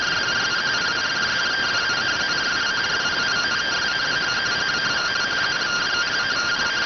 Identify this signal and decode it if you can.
Two-tone teleprinter signal, about 170 Hz shift.
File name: findings / CT AXIAL -572mm 2.gen-38 rx